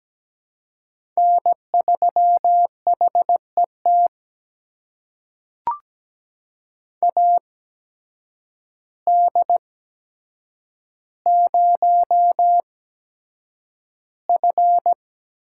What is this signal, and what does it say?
Morse code, heard as a keyed tone, about 17 wpm, spaced as for 8 wpm, 700 Hz. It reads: N3HET AD0F
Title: USA Call Signs - 8x Character Spacing - 17wpm